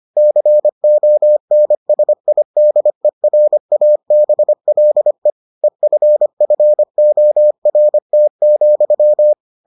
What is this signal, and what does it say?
Morse code, heard as a keyed tone, about 25 wpm, 600 Hz. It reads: CONSIDERABLE EFFORT,